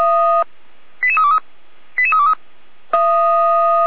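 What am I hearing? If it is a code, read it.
Five-tone selective call (ZVEI), heard as quick runs of five tones, tones of about 70 ms. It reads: 88323 88323
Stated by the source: ZVEI five-tone selective call 88417 received by SDR, from the Feuerwehr Orlamünde fire brigade server.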